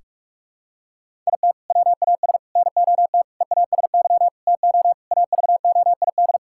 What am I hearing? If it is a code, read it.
Morse code, heard as a keyed tone, about 46 wpm, 700 Hz. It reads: IT WAS NOT EASY TO AVOID
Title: Sentences from Top 1000 Words - Review Part 6 - 45wpm